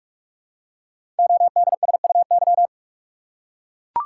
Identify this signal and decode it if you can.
Morse code, heard as a keyed tone, about 44 wpm, 700 Hz. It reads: ODSUY
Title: Code Groups - 5 Letters - 45wpm